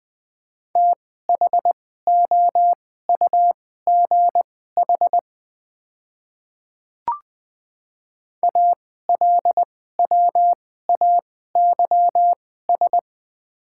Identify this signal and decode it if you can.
Morse code, heard as a keyed tone, about 20 wpm, spaced as for 14 wpm, 690 Hz. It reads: THOUGH ALWAYS
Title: 2x Character Spacing - Top 500 Words - 20wpm